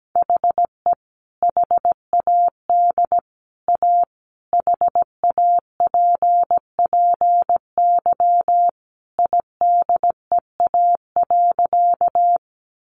Morse code, heard as a keyed tone, about 17 wpm, 700 Hz. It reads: HE HAD A HAPPY IDEA.
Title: Sentences from Top 1000 Words 17wpm